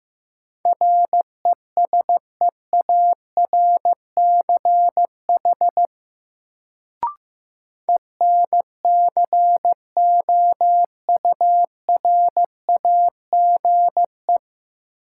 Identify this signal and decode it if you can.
Morse code, heard as a keyed tone, about 15 wpm, 700 Hz. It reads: RESEARCH ENCOURAGE
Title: Sentences from Top 1400 Words - New Word Drill Alt - 15wpm